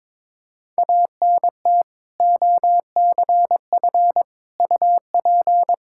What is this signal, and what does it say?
Morse code, heard as a keyed tone, about 22 wpm, 700 Hz. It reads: ANT OCF UP
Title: Practice QSO - Exchange 2 - Info - 22wpm